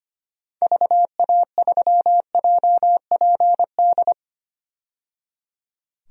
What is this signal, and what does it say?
Morse code, heard as a keyed tone, about 25 wpm, 700 Hz. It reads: VA3JPD
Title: DXCC-100 Call Signs - 6 Characters - 25wpm